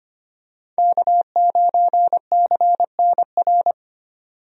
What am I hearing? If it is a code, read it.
Morse code, heard as a keyed tone, about 25 wpm, 690 Hz. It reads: K9CNR
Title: USA Call Signs - 5 Characters - 25wpm